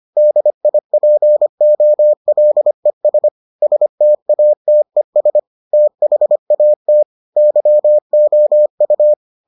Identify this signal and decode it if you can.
Morse code, heard as a keyed tone, about 25 wpm, 600 Hz. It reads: DIPOLES STATES THAT YOU